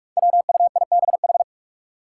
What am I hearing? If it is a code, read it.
Morse code, heard as a keyed tone, about 46 wpm, 700 Hz. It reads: WUIBH
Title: Five Challenging Letter-Number Combinations 45wpm